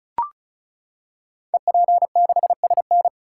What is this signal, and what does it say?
Morse code, heard as a keyed tone, about 36 wpm, 690 Hz. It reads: EP6SN
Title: Code Groups - 5 Letter:Numbers - 35wpm